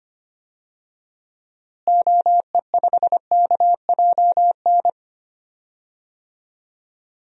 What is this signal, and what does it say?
Morse code, heard as a keyed tone, about 25 wpm, 700 Hz. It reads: OE5KJN
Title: DXCC-100 Call Signs - 6 Characters - Repeated 3x - 25wpm